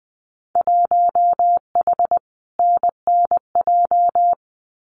Morse code, heard as a keyed tone, about 20 wpm, 700 Hz. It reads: 1H NNJ
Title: Winter Field Day Exchanges 20wpm